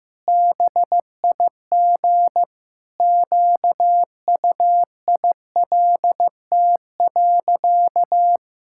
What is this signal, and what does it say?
Morse code, heard as a keyed tone, about 16 wpm, 700 Hz. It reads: BIG QUILT.
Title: Pangram Challenge v2 - 15wpm